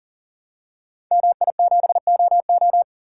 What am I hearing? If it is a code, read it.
Morse code, heard as a keyed tone, about 41 wpm, 700 Hz. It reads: MI7OO
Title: Code Groups - 5 Letter:Numbers - 40wpm